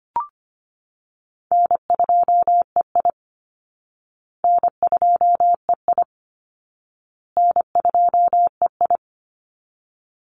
Morse code, heard as a keyed tone, about 25 wpm, 700 Hz. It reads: N2EI N2EI N2EI
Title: Call Signs Repeated 2x - 25wpm